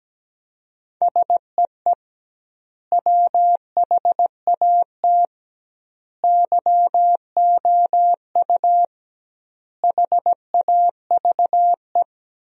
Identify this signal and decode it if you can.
Morse code, heard as a keyed tone, about 17 wpm, 700 Hz. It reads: SEE WHAT YOU HAVE
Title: 2x Word-Spacing - Sentences from Top 500 Words - 17wpm